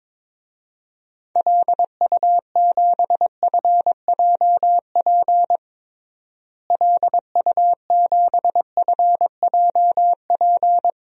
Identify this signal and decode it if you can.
Morse code, heard as a keyed tone, about 22 wpm, 700 Hz. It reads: LU7FJP LU7FJP
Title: DXCC-100 Call Signs - 6 Characters - Repeated 3x - 22wpm